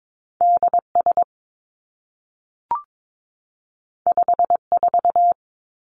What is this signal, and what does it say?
Morse code, heard as a keyed tone, about 22 wpm, 700 Hz. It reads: D S 5 4
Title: Two Challenging Letter-Number Combinations 22wpm